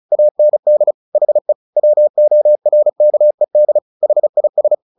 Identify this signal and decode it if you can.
Morse code, heard as a keyed tone, about 34 wpm, 600 Hz. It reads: AND HE WORKED HIS
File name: the old man and the sea / the old man and the sea 1 35wpm